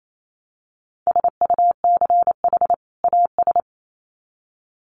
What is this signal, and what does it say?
Morse code, heard as a keyed tone, about 28 wpm, 700 Hz. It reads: SUCH AS
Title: Sets of 2 Words 28wpm